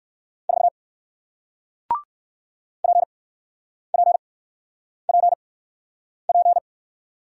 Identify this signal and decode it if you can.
Morse code, heard as a keyed tone, about 56 wpm, spaced as for 9 wpm, 700 Hz. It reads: 4 PPPP
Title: Speed-Racing Single Letter-Number 45wpm